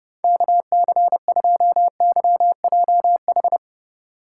Morse code, heard as a keyed tone, about 27 wpm, 700 Hz. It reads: KC2YJH